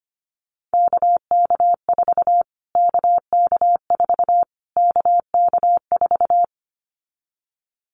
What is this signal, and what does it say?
Morse code, heard as a keyed tone, about 25 wpm, 700 Hz. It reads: KK4 KK4 KK4
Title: Callsign Prefixes for Top 25 DXCC - Repeated 3x - 25wpm